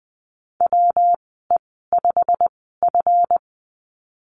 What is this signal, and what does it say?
Morse code, heard as a keyed tone, about 20 wpm, spaced as for 15 wpm, 700 Hz. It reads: WE5F
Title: 161 Lesson 26g - Letter g - Call Signs 20wpm